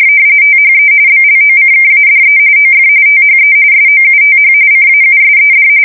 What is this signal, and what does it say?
RTTY, about 45.45 baud. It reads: THE BLOCKADE OF THE SOUTH CAUSED
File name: fsk170 / usnavy3